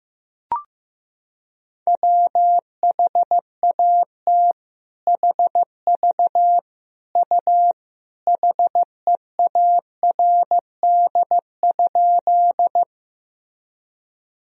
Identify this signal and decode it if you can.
Morse code, heard as a keyed tone, about 15 wpm, 700 Hz. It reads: WHAT HV U HEARD?
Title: Sentences from Top 400 Words and Abbreviations 15wpm